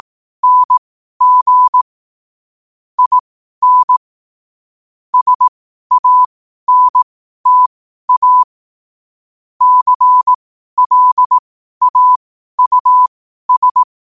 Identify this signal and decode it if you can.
Morse code, heard as a keyed tone, about 18 wpm, spaced as for 12 wpm, 1000 Hz. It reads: NG IN SANTA CLAUS